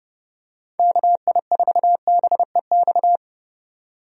Morse code, heard as a keyed tone, about 30 wpm, 700 Hz. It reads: KI4BEX
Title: USA Call Signs - 6 Characters - 30wpm